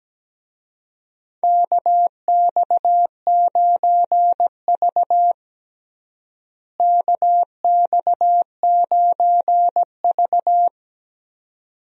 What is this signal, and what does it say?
Morse code, heard as a keyed tone, about 17 wpm, 700 Hz. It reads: KX9V KX9V
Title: USA Call Signs - 4 Characters - Repeated 2x - 17wpm